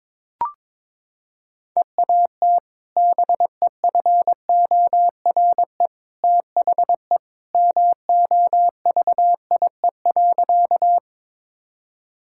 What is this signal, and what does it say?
Morse code, heard as a keyed tone, about 23 wpm, 700 Hz. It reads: EAT BEFORE THE MOVIE.
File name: practice / 22wpm / Sentences with 2-Word Phrases - 22wpm